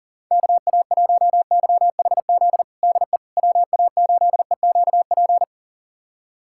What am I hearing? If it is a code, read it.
Morse code, heard as a keyed tone, about 40 wpm, 700 Hz. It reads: KA1YHZ DE WA8EQP
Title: USA Callsign DE Callsign - 6 Characters - 40wpm